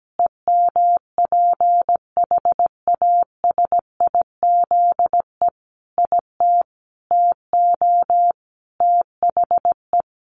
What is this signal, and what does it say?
Morse code, heard as a keyed tone, about 17 wpm, 700 Hz. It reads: EMPHASIZE IT TO THE
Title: Sentences from Top 1500 Words 17wpm